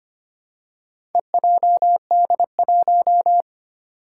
Morse code, heard as a keyed tone, about 25 wpm, 700 Hz. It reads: EJD1
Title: Four Challenging Letter-Number Combinations 25wpm